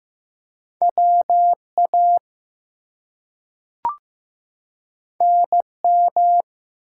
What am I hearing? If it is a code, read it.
Morse code, heard as a keyed tone, about 15 wpm, 700 Hz. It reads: WA NM
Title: US States and Canadian Provinces 15wpm